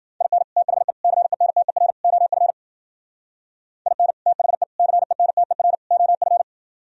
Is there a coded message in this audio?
Yes: Morse code from 0.2 to 6.4 s, heard as a keyed tone, about 51 wpm, 700 Hz. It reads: IN THE CENTER OF IN THE CENTER OF